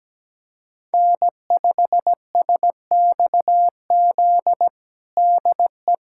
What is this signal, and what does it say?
Morse code, heard as a keyed tone, about 17 wpm, 700 Hz. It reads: N5SXZ DE
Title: USA Callsign DE Callsign 17wpm